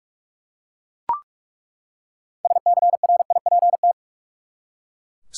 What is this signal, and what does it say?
Morse code, heard as a keyed tone, about 46 wpm, 700 Hz. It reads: SCRIPT
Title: Six Letter Words - 45wpm